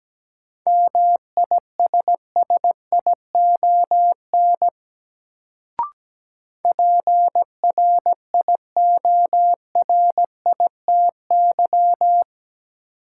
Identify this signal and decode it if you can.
Morse code, heard as a keyed tone, about 17 wpm, 700 Hz. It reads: MISSION PRIORITY
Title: Sentences from Top 1600 Words - New Word Drill Alt - 17wpm